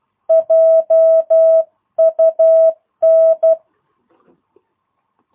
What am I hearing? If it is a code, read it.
Morse code, heard as a keyed tone, about 12 wpm, 650 Hz. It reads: JUN
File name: jh8cbh / 2022cw